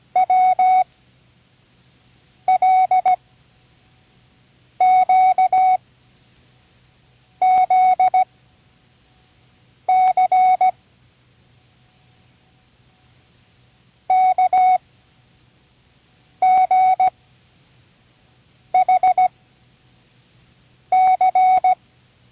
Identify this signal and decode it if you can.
Morse code, heard as a keyed tone, about 17 wpm, spaced as for 5 wpm, 740 Hz. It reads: WLQZC KGHC
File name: Learning Morse Code (from Tune in the World)